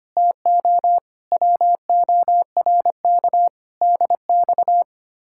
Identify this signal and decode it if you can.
Morse code, heard as a keyed tone, about 28 wpm, 700 Hz. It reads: TO WORK DX
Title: Speed-Racing Common QSO Phrases - 25wpm